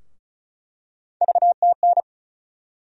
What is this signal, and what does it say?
Morse code, heard as a keyed tone, about 34 wpm, 700 Hz. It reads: VTN
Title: USA Call Sign Postfixes - 3 Letters 35wpm